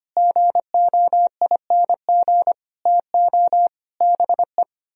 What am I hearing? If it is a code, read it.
Morse code, heard as a keyed tone, about 30 wpm, 700 Hz. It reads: GOING TO BE
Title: Speed-Racing - Sets of 5 Words 25wpm